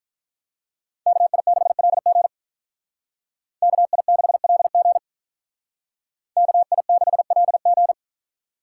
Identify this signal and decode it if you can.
Morse code, heard as a keyed tone, about 44 wpm, 700 Hz. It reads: KI6LG KI6LG KI6LG